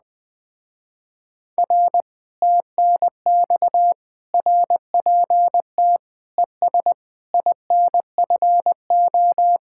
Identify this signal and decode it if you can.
Morse code, heard as a keyed tone, about 20 wpm, 700 Hz. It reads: R TNX RPT ES INFO
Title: Speed-Racing Common QSO Phrases Encore - 17wpm